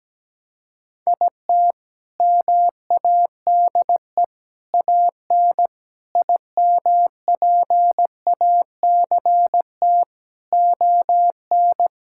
Morse code, heard as a keyed tone, about 17 wpm, 700 Hz. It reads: IT MADE AN IMPACT ON
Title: Sentences from Top 1000 Words - Review Part 6 - 17wpm